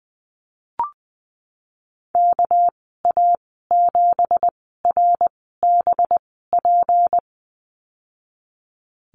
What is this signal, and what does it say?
Morse code, heard as a keyed tone, about 20 wpm, spaced as for 15 wpm, 700 Hz. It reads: KA7RBP
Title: 231 Lesson 37g - Letter x - Call Signs 20wpm